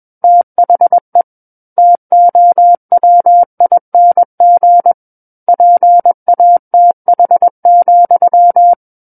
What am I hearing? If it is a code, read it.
Morse code, heard as a keyed tone, about 21 wpm, 700 Hz. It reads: THE TOWING PATH,
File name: Book One / 12-Chapter 12 - 21 WPM